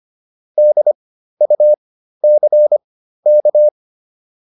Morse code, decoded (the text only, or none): DUCK